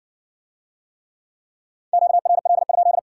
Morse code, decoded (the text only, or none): QRL?